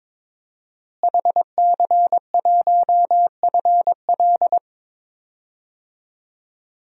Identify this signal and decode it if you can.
Morse code, heard as a keyed tone, about 22 wpm, 700 Hz. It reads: HC1FL
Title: DXCC-100 Call Signs - 5 Characters - Repeated 2x - 22wpm